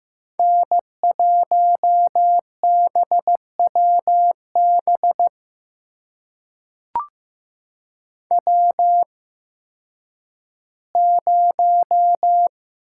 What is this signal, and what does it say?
Morse code, heard as a keyed tone, about 15 wpm, spaced as for 10 wpm, 700 Hz. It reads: N1BWB W0
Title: USA Call Signs - 5 Characters - 8x Character Spacing - 15wpm